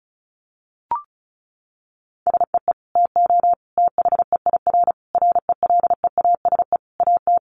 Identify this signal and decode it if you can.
Morse code, heard as a keyed tone, about 34 wpm, 700 Hz. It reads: SEE TO THEIR RELEASE AT
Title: Sentences from Top 1100 Words 35wpm